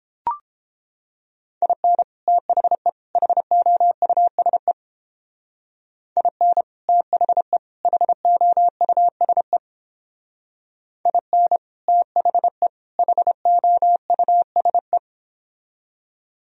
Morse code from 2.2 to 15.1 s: THE HOUSE IN THE HOUSE IN THE HOUSE

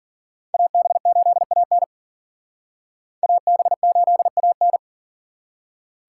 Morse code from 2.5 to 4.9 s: AB8AN

